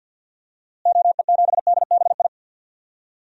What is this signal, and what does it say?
Morse code, heard as a keyed tone, about 52 wpm, 700 Hz. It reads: OE7DBI
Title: DXCC-100 Call Signs - 6 Characters - Repeated 3x - 50wpm